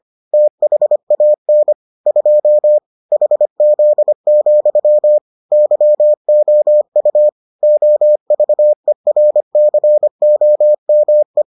Morse code, heard as a keyed tone, about 25 wpm, 600 Hz. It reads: THAN 2 HZ, YOU OVERCOME